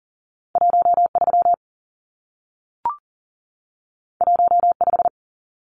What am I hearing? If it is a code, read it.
Morse code, heard as a keyed tone, about 40 wpm, 700 Hz. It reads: 13 15
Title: Numbers - 2-Digit 40wpm